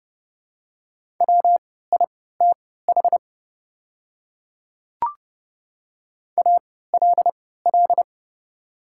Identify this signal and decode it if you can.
Morse code, heard as a keyed tone, about 30 wpm, spaced as for 17 wpm, 700 Hz. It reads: WITH ALL
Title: 3x Character Spacing - Top 100 Words - 30wpm